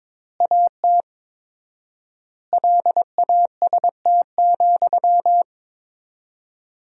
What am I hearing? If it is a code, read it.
Morse code, heard as a keyed tone, about 21 wpm, 700 Hz. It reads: AT LAST,
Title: 4x Word-Spacing - Sentences from Top 300 Words - 22wpm